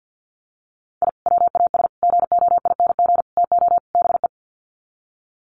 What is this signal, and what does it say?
Morse code, heard as a keyed tone, about 50 wpm, 700 Hz. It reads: I WAS GOING TO BE
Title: Sets of 5 Words 50wpm